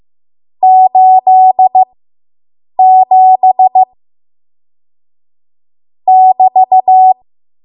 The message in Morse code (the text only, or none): 87 <BT>